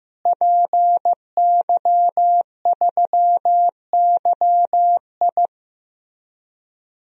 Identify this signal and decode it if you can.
Morse code, heard as a keyed tone, about 15 wpm, 700 Hz. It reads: PY3YI
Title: DXCC-100 Call Signs - 5 Characters - Repeated 3x - 15wpm